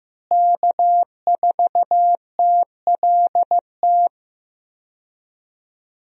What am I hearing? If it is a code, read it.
Morse code, heard as a keyed tone, about 15 wpm, 700 Hz. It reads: K4TLT
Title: USA Call Signs - 5 Characters - Repeated 3x - 15wpm